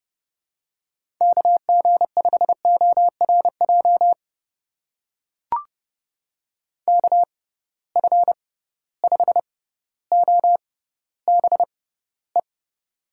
Morse code, heard as a keyed tone, about 30 wpm, spaced as for 15 wpm, 700 Hz. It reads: KG5ORJ KF5OBE
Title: USA Call Signs - 6 Characters - 6x Character Spacing - 30wpm